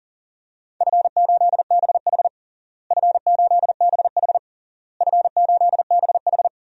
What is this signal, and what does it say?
Morse code, heard as a keyed tone, about 40 wpm, 700 Hz. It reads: F8BH F8BH F8BH
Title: DXCC-100 Call Signs - 4 Characters - Repeated 3x - 40wpm